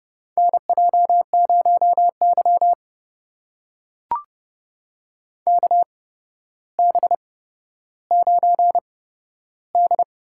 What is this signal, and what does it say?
Morse code, heard as a keyed tone, about 30 wpm, spaced as for 13 wpm, 700 Hz. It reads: NJ0Y KB9D
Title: USA Call Signs - 4 Characters - 8x Character Spacing - 30wpm